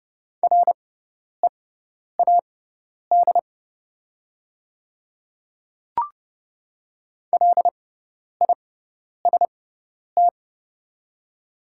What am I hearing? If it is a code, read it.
Morse code, heard as a keyed tone, about 30 wpm, spaced as for 10 wpm, 700 Hz. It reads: READ LIST
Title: 6x Character Spacing - Top 500 Words - 30wpm